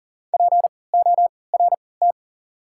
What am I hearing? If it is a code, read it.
Morse code, heard as a keyed tone, about 40 wpm, spaced as for 23 wpm, 700 Hz. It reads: PORT
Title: 3x Character Spacing - Top 200 Words - 40wpm